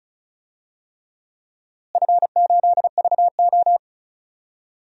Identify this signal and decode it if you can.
Morse code, heard as a keyed tone, about 34 wpm, 700 Hz. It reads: F8VO
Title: DXCC-100 Call Signs - 4 Characters - Repeated 3x - 35wpm